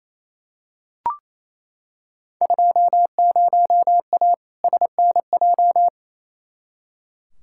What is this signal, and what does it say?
Morse code, heard as a keyed tone, about 28 wpm, 700 Hz. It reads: 20A SNJ
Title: Field Day Exchanges Encore - 28wpm